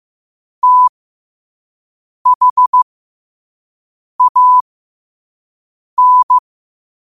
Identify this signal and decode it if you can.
Morse code, heard as a keyed tone, about 15 wpm, spaced as for 5 wpm, 1000 Hz. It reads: THAN
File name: qotdfiles / 5wpm